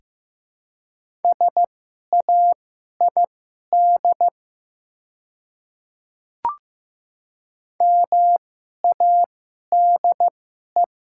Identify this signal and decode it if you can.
Morse code, heard as a keyed tone, about 15 wpm, spaced as for 11 wpm, 700 Hz. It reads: SAID MADE
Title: 2x Character Spacing - Top 300 Words - 15wpm